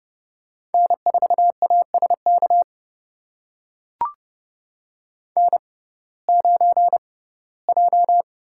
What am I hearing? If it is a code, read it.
Morse code, heard as a keyed tone, about 31 wpm, spaced as for 19 wpm, 700 Hz. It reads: N4ASK N9J
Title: USA Call Signs - 5 Characters - 6x Character Spacing - 30wpm